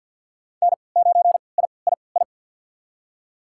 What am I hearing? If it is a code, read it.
Morse code, heard as a keyed tone, about 53 wpm, spaced as for 29 wpm, 700 Hz. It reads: N9III
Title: USA Call Signs - 5 Characters - 3x Character Spacing - 50wpm